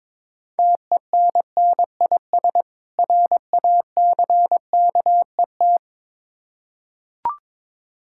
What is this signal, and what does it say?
Morse code, heard as a keyed tone, about 22 wpm, 700 Hz. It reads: TENNIS RACKET
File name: Difficult Words with Hints - 22wpm